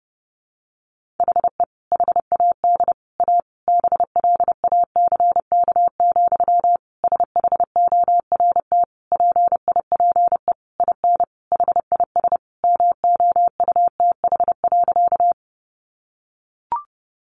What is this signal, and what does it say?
Morse code, heard as a keyed tone, about 30 wpm, 700 Hz. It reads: HE HAD A BLACK, SHORT PIPE IN HIS MOUTH.
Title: Sentences from Top 1500 Words 30wpm